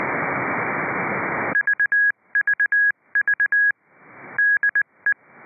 Morse code, heard as a keyed tone, about 20 wpm, 1650 Hz. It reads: VVV DE